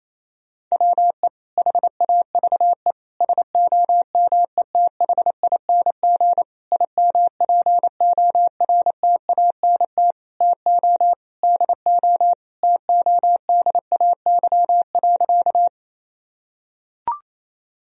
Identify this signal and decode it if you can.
Morse code, heard as a keyed tone, about 28 wpm, 700 Hz. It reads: WE HAVE SOMETHING IMPORTANT TO DO TODAY.
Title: Sentences from Top 1000 Words - Review Part 4 - 28wpm